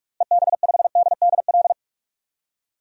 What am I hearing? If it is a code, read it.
Morse code, heard as a keyed tone, about 44 wpm, 700 Hz. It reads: EB5DDL
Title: DXCC-100 Call Signs - 6 Characters - Repeated 2x - 45wpm